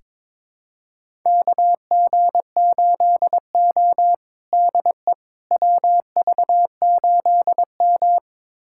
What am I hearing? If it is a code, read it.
Morse code, heard as a keyed tone, about 21 wpm, 700 Hz. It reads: KG8O DE WV8M
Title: USA Callsign DE Callsign - 4 Characters - 22wpm